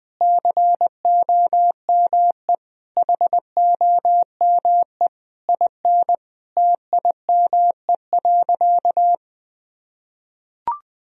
Morse code, heard as a keyed tone, about 20 wpm, 700 Hz. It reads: COME HOME IN TIME.
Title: Sentences from Top 1000 Words - Review Part 3 - 20wpm